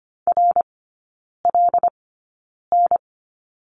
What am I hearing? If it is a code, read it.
Morse code, heard as a keyed tone, about 25 wpm, spaced as for 9 wpm, 700 Hz. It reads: RLN
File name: Lesson 15e - Number 5 - Three Characters 25wpm